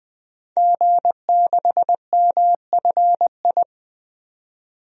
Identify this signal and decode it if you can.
Morse code, heard as a keyed tone, about 20 wpm, 700 Hz. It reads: G6MFI
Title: DXCC-100 Call Signs - 5 Characters - 20wpm